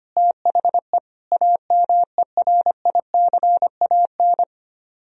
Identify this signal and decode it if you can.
Morse code, heard as a keyed tone, about 33 wpm, 700 Hz. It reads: THE AMERICAN